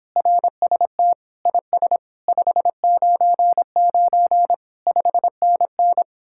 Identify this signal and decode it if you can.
Morse code, heard as a keyed tone, about 26 wpm, 700 Hz. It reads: RST IS 599 5NN